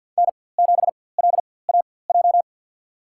Morse code, heard as a keyed tone, about 53 wpm, spaced as for 24 wpm, 700 Hz. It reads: N7LAJ
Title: USA Call Signs - 5 Characters - 4x Character Spacing - 50wpm